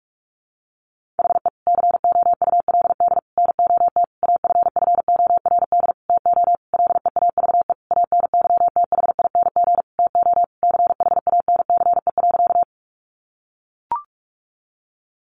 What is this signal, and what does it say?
Morse code, heard as a keyed tone, about 44 wpm, 700 Hz. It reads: HE COULD NOT AFFORD TO LEAVE ANYTHING TO CHANCE.
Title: Sentences from Top 1100 Words 45wpm